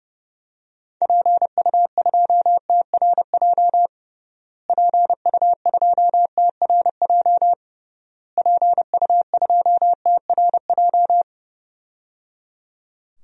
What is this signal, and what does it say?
Morse code, heard as a keyed tone, about 30 wpm, 700 Hz. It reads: PU2TRJ PU2TRJ PU2TRJ
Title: DXCC-100 Call Signs - 6 Characters - Repeated 3x - 30wpm